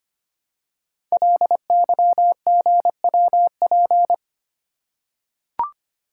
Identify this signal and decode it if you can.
Morse code, heard as a keyed tone, about 25 wpm, 700 Hz. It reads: LYGWP
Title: Code Groups - 5 Letter:Numbers - 25wpm